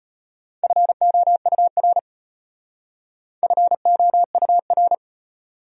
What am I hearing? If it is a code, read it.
Morse code, heard as a keyed tone, about 32 wpm, 700 Hz. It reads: FOUR FOUR